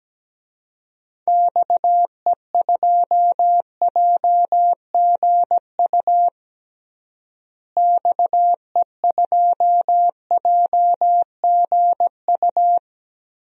Morse code, heard as a keyed tone, about 17 wpm, 700 Hz. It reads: XE2JGU XE2JGU
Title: DXCC-100 Call Signs - 6 Characters - Repeated 2x - 17wpm